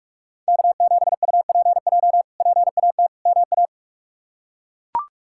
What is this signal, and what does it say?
Morse code, heard as a keyed tone, about 41 wpm, 700 Hz. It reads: K7UPJ PAT MA
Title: K1USN SST QSO Practice - 45wpm